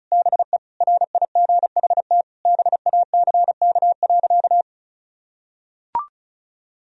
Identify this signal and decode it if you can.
Morse code, heard as a keyed tone, about 36 wpm, 700 Hz. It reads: BE RIGHT BACK.
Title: Sentences from Top 1000 Words - Review Part 4 - 35wpm